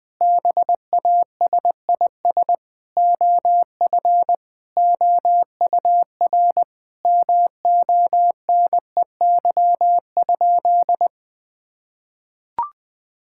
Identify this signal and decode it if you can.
Morse code, heard as a keyed tone, about 21 wpm, 700 Hz. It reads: BASIS OF OUR MONEY?
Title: Sentences from Top 1000 Words - Review Part 5 - 20wpm